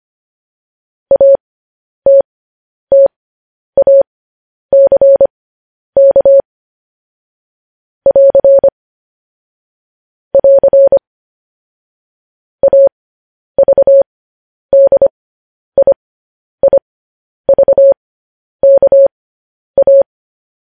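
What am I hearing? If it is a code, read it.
Morse code, heard as a keyed tone, about 25 wpm, spaced as for 10 wpm, 550 Hz. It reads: ATTACK <AR> <AR> AVDIIVKA
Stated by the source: Morse code news headlines sent at 10 wpm.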